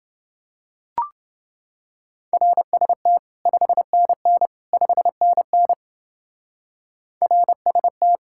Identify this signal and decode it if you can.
Morse code, heard as a keyed tone, about 28 wpm, 700 Hz. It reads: RST 5NN 5NN RST